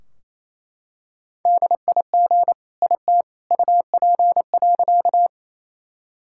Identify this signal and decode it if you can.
Morse code, heard as a keyed tone, about 28 wpm, 700 Hz. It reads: DIG IT UP.
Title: Sentences from Top 1600 Words 28wpm